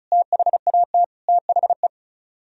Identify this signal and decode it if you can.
Morse code, heard as a keyed tone, about 43 wpm, 700 Hz. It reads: THAT THE
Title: Speed-Racing - Sets of 3 Words 35wpm